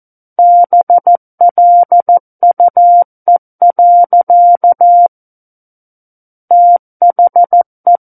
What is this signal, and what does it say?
Morse code, heard as a keyed tone, about 14 wpm, 700 Hz. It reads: BLUE. THE
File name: Book One / 05-Chapter 5 - 14 WPM